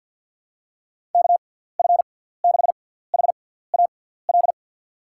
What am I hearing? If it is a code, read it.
Morse code, heard as a keyed tone, about 51 wpm, spaced as for 17 wpm, 700 Hz. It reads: KF6HAL
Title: USA Call Signs - 6x Character Spacing - 50wpm